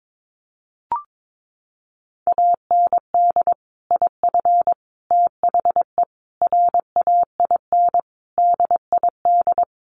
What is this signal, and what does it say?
Morse code, heard as a keyed tone, about 22 wpm, 700 Hz. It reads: AND IF THE RAIN DID
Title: Sentences from Top 1300 Words 22wpm